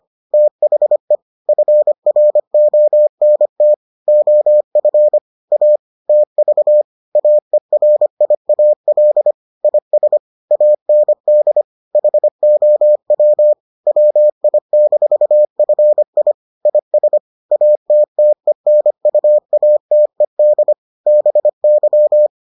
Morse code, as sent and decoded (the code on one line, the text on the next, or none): - .... / ..-. .-. --- -. - / --- ..-. / .- / - ...- / .- . .-. .. .- .-.. / .. ... / .- -. -.. / .... --- .-- / .-- .. -....- ..-. .. / .. ... / .- - - . -. ..- .- - . -.. / -... -.--
THE FRONT OF A TV AERIAL IS AND HOW WI-FI IS ATTENUATED BY